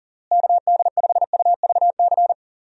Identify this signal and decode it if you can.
Morse code, heard as a keyed tone, about 37 wpm, 700 Hz. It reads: KD5UVC
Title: USA Call Signs - 6 Characters - Repeated 3x - 40wpm